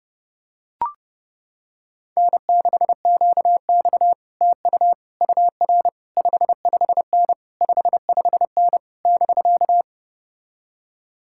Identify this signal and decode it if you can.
Morse code, heard as a keyed tone, about 31 wpm, 700 Hz. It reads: N6QX TU UR 55N 55N <BK>